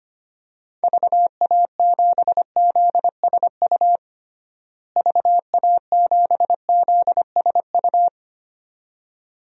Morse code, heard as a keyed tone, about 25 wpm, 700 Hz. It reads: VA7ZSU VA7ZSU
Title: DXCC-100 Call Signs - 6 Characters - Repeated 2x - 25wpm